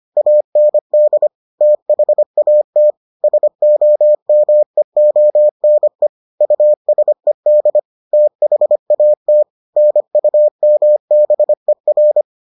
Morse code, heard as a keyed tone, about 25 wpm, 600 Hz. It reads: AND THAT SOMEONE USED THAT NUMBER